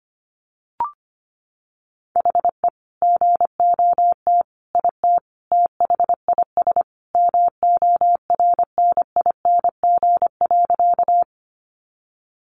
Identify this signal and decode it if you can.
Morse code, heard as a keyed tone, about 25 wpm, 700 Hz. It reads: HE GOT IT THIS MORNING.
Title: Sentences from Top 400 Words 25wpm